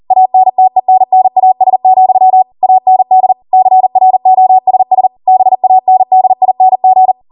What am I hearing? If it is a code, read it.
Morse code, heard as a keyed tone, about 40 wpm, 750 Hz. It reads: ANTENNAS, AND CROSS BANDING